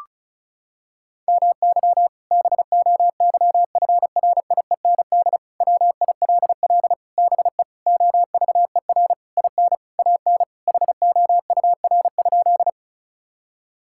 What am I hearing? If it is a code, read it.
Morse code, heard as a keyed tone, about 34 wpm, 700 Hz. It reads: MY BOYFRIEND WILL BE OVER IN AN HOUR?
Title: Sentences from Top 1000 Words - Review Part 5 - 35wpm